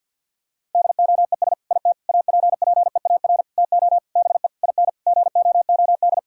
Morse code, heard as a keyed tone, about 50 wpm, 700 Hz. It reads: DOES IT APPEAR TO BE IN GOOD
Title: Sentences from Top 1000 Words - Review Part 6 - 50wpm